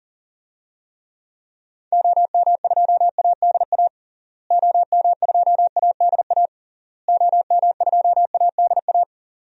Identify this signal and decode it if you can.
Morse code, heard as a keyed tone, about 37 wpm, 700 Hz. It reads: OM2ADA OM2ADA OM2ADA
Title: DXCC-100 Call Signs - 6 Characters - Repeated 3x - 40wpm